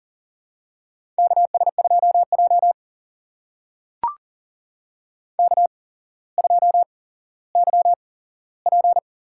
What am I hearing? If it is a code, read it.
Morse code, heard as a keyed tone, about 40 wpm, spaced as for 17 wpm, 700 Hz. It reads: KS2J K2YP
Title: USA Call Signs - 4 Characters - 8x Character Spacing - 40wpm